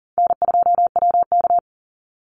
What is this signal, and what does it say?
Morse code, heard as a keyed tone, about 38 wpm, 700 Hz. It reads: N2WK
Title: Call Signs - 40wpm